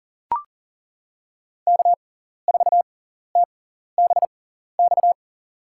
Morse code, heard as a keyed tone, about 42 wpm, spaced as for 14 wpm, 700 Hz. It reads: K4TBX
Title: USA Call Signs - 5 Characters - 6x Character Spacing - 40wpm